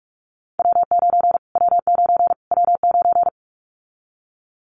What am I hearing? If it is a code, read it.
Morse code, heard as a keyed tone, about 44 wpm, 700 Hz. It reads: W9 W9 W9
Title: Callsign Prefixes for Top 25 DXCC - Repeated 3x - 45wpm